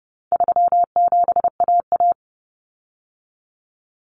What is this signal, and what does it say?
Morse code, heard as a keyed tone, about 30 wpm, 700 Hz. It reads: 37AA
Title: Common Cut Numbers - 4-Digit 30wpm